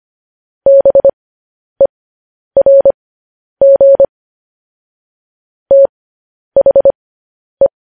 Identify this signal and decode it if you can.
Morse code, heard as a keyed tone, about 25 wpm, spaced as for 10 wpm, 550 Hz. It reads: BERG THE